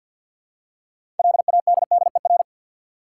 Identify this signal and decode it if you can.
Morse code, heard as a keyed tone, about 52 wpm, 700 Hz. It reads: LADDER
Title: Six Letter Words - 50wpm